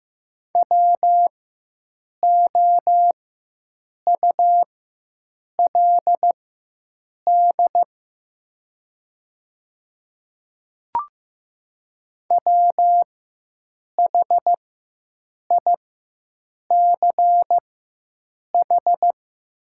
Morse code, heard as a keyed tone, about 15 wpm, spaced as for 7 wpm, 700 Hz. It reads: WOULD WHICH
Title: 4x Character Spacing - Top 200 Words - 15wpm